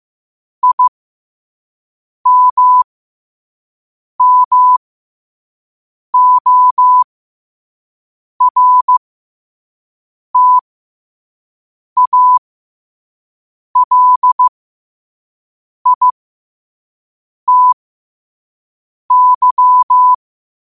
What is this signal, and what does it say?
Morse code, heard as a keyed tone, about 15 wpm, spaced as for 5 wpm, 1000 Hz. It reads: IMMORTALITY
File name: qotdfiles / 5wpm